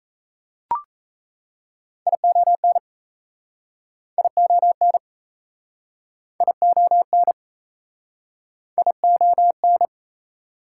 Morse code, heard as a keyed tone, about 39 wpm, 700 Hz. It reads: ION ION ION ION